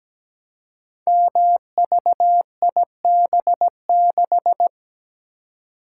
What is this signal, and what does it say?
Morse code, heard as a keyed tone, about 17 wpm, 700 Hz. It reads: MVIB6
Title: Code Groups - 5 Letter:Numbers - 17wpm